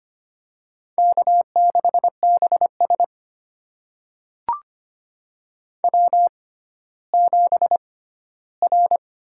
Morse code, heard as a keyed tone, about 25 wpm, spaced as for 14 wpm, 700 Hz. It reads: K6BS W7R
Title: USA Call Signs - 4 Characters - 6x Character Spacing - 25wpm